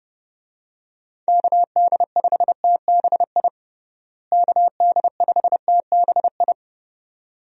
Morse code, heard as a keyed tone, about 30 wpm, 700 Hz. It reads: KD5TBI KD5TBI
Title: USA Call Signs - 6 Characters - Repeated 2x - 30wpm